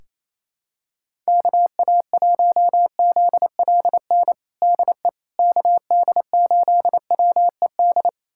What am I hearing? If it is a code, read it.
Morse code, heard as a keyed tone, about 28 wpm, 700 Hz. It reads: KA1ZLN DE KD8WED
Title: USA Callsign DE Callsign - 6 Characters - 28wpm